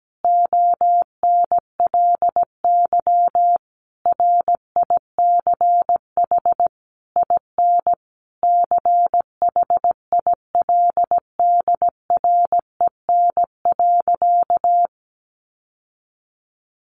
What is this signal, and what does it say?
Morse code, heard as a keyed tone, about 17 wpm, 700 Hz. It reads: ONLY RICH IN CHILDREN.
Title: Sentences from Top 1100 Words 17wpm